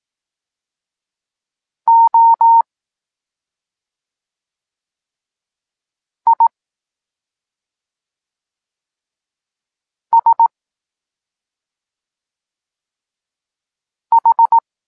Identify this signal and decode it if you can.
Morse code, heard as a keyed tone, about 18 wpm, spaced as for 2 wpm, 920 Hz. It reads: OISH
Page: Tekst 8 – doskonalenie odbioru poznanych liter